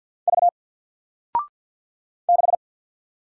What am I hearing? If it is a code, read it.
Morse code, heard as a keyed tone, about 48 wpm, 700 Hz. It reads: V 6
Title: Single Letter-Number Alt 50wpm